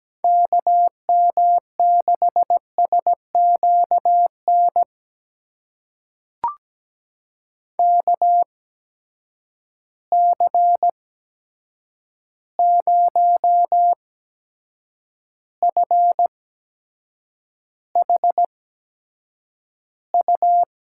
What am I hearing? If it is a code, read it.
Morse code, heard as a keyed tone, about 17 wpm, spaced as for 7 wpm, 700 Hz. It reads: KM6SQN KC0FHU
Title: USA Call Signs - 6 Characters - 8x Character Spacing - 17wpm